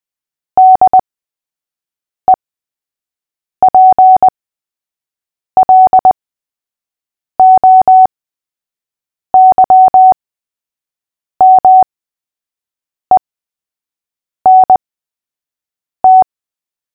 Morse code, heard as a keyed tone, about 20 wpm, spaced as for 6 wpm, 750 Hz. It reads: DEPLOYMENT
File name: cw / fists / Short-Story-3-6wpm